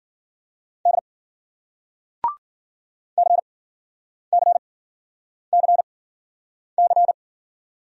Morse code, heard as a keyed tone, about 54 wpm, spaced as for 9 wpm, 700 Hz. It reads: D CCCC